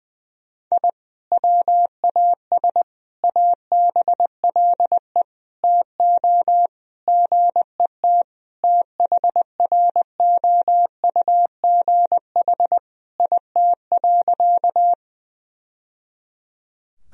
Morse code, heard as a keyed tone, about 21 wpm, 700 Hz. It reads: I WAS ABLE TO GET THROUGH IT.
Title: Sentences from Top 1000 Words - Review Part 2 - 20wpm